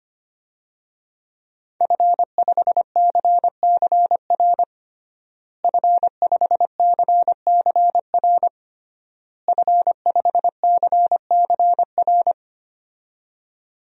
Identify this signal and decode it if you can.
Morse code, heard as a keyed tone, about 25 wpm, 700 Hz. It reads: F5CCR F5CCR F5CCR
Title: DXCC-100 Call Signs - 5 Characters - Repeated 3x - 25wpm